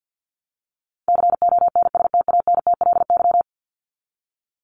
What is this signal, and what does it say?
Morse code, heard as a keyed tone, about 50 wpm, 700 Hz. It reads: CONSTANTLY
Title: Top 751-1000 Words 50wpm